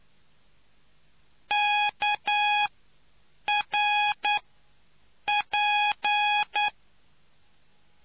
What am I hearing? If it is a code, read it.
Morse code, heard as a keyed tone, about 9 wpm, 830 Hz. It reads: K R P